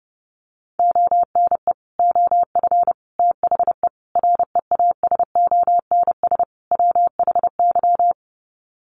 Morse code, heard as a keyed tone, about 30 wpm, 700 Hz. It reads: ONE OF THE REASONS WHY
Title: Sets of 5 Words 30wpm